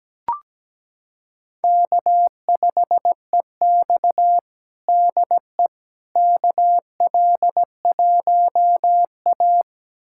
Morse code, heard as a keyed tone, about 17 wpm, 700 Hz. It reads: K5EX DE KL1A
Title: USA Callsign DE Callsign - 4 Characters - 17wpm